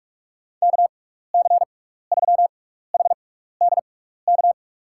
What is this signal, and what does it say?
Morse code, heard as a keyed tone, about 44 wpm, spaced as for 15 wpm, 700 Hz. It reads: KC3HDK
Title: USA Call Signs - 6 Characters - 6x Character Spacing - 45wpm